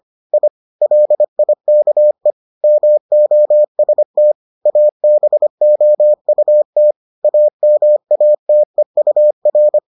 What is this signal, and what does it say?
Morse code, heard as a keyed tone, about 25 wpm, 600 Hz. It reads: I LIKE MOST ABOUT AMATEUR